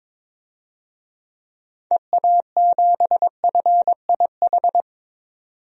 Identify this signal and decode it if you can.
Morse code, heard as a keyed tone, about 22 wpm, 700 Hz. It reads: EA7FIH
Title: DXCC-100 Call Signs - 6 Characters - 22wpm